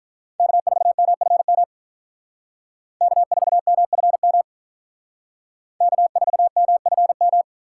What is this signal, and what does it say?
Morse code, heard as a keyed tone, about 44 wpm, 700 Hz. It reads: K4MFM K4MFM K4MFM